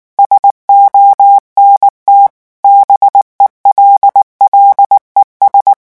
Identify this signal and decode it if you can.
Morse code, heard as a keyed tone, about 19 wpm, 800 Hz. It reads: SONT BELLES
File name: morse mp3 / revision9 citations humoristiques vitesse 18 mots minute